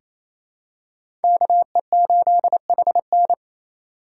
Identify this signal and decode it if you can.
Morse code, heard as a keyed tone, about 28 wpm, 700 Hz. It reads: KE8HN
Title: USA Call Signs - 5 Characters - 4x Character Spacing - 28wpm